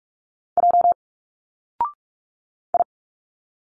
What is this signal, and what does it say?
Morse code, heard as a keyed tone, about 45 wpm, 700 Hz. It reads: J I
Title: Single Letters 45wpm